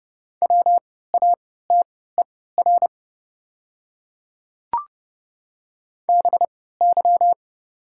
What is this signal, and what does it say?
Morse code, heard as a keyed tone, about 31 wpm, spaced as for 17 wpm, 700 Hz. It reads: WATER BY
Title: 3x Character Spacing - Top 100 Words - 30wpm